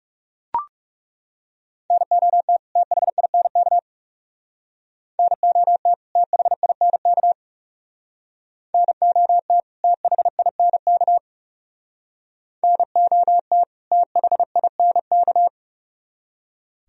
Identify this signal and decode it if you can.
Morse code, heard as a keyed tone, about 37 wpm, 700 Hz. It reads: NOT THINK NOT THINK NOT THINK NOT THINK
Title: Speed-Racing - Sets of 2 Words 30wpm